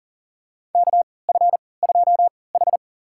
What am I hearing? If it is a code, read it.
Morse code, heard as a keyed tone, about 37 wpm, spaced as for 22 wpm, 700 Hz. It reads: KF2H
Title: USA Call Signs - 4 Characters - 3x Character Spacing - 40wpm